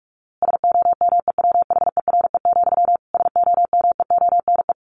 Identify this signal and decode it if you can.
Morse code, heard as a keyed tone, about 44 wpm, 700 Hz. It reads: SOMEWHERE, SOMEONE
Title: A Sound Of Thunder 45wpm